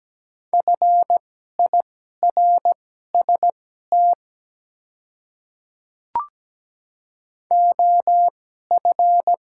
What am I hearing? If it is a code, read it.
Morse code, heard as a keyed tone, about 17 wpm, spaced as for 12 wpm, 700 Hz. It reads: FIRST OF
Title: 2x Character Spacing - Top 200 Words - 17wpm